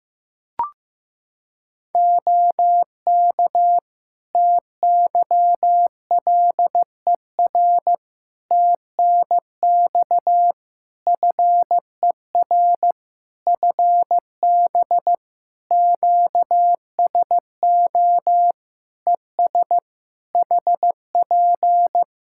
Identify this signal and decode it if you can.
Morse code, heard as a keyed tone, about 15 wpm, 700 Hz. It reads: OK TYLER TNX FER FB QSO ES HP
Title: Practice QSO - Exchange 4 - Ending - 15wpm